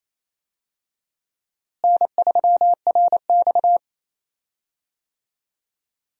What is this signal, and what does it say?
Morse code, heard as a keyed tone, about 28 wpm, 700 Hz. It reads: N3RX
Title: USA Call Signs - 4 Characters - Repeated 3x - 28wpm